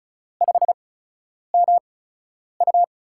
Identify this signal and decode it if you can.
Morse code, heard as a keyed tone, about 36 wpm, spaced as for 10 wpm, 700 Hz. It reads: 5MU
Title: USA Call Signs - 4 Characters - 8x Character Spacing - 35wpm